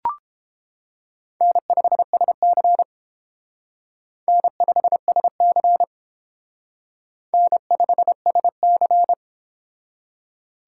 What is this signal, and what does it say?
Morse code, heard as a keyed tone, about 28 wpm, 700 Hz. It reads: N5SC N5SC N5SC